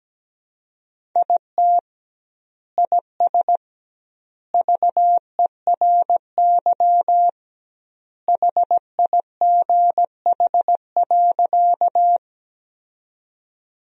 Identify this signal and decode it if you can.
Morse code, heard as a keyed tone, about 17 wpm, 700 Hz. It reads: IT IS VERY HIGH.
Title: 2x Word-Spacing - Sentences from Top 200 Words - 17wpm